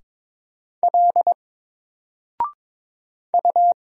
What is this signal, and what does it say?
Morse code, heard as a keyed tone, about 22 wpm, 700 Hz. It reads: L U
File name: Single-Letter-Number-22wpm